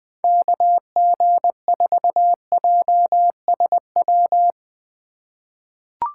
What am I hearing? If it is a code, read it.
Morse code, heard as a keyed tone, about 21 wpm, 700 Hz. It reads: KG4JSW
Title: USA Call Signs - 6 Characters - 2x Character Spacing - 20wpm